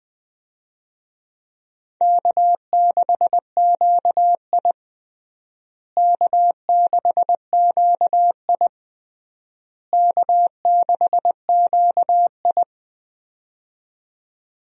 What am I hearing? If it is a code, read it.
Morse code, heard as a keyed tone, about 20 wpm, 700 Hz. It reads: K6QI K6QI K6QI
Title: USA Call Signs - 4 Characters - Repeated 3x - 20wpm